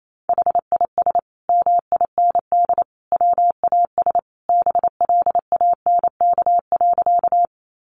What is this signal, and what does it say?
Morse code, heard as a keyed tone, about 28 wpm, 700 Hz. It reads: HIS MIND WAS BLANK.
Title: Sentences from Top 1500 Words 28wpm